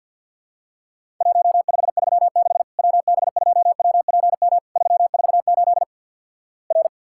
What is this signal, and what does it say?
Morse code, heard as a keyed tone, about 50 wpm, 700 Hz. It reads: 1536 WB2WPM 348 R
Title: Worked All Europe DX Contest - QTC Exchange with Repeats - 50wpm